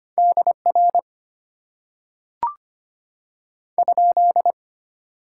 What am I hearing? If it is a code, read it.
Morse code, heard as a keyed tone, about 25 wpm, 700 Hz. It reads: DR ?
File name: Top 90 QSO Elements - 8x Character Spacing - 25wpm